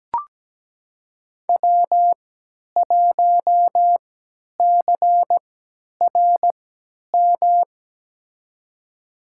Morse code, heard as a keyed tone, about 17 wpm, spaced as for 10 wpm, 700 Hz. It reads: W1CRM
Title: USA Call Signs - 5 Characters - 3x Character Spacing - 17wpm